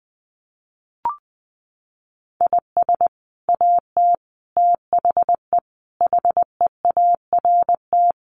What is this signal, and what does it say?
Morse code, heard as a keyed tone, about 20 wpm, 700 Hz. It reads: IS AT THE HEART
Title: Sets of 5 Words Encore2 20wpm